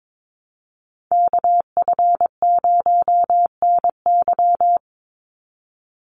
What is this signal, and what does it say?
Morse code, heard as a keyed tone, about 22 wpm, 700 Hz. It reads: KF0NY
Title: USA Call Signs 22wpm